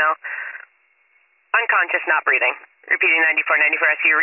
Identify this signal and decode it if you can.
Emergency Alert System (SAME) tones, heard as harsh buzzing bursts, with speech. It is heard from 0.2 to 3.9 s.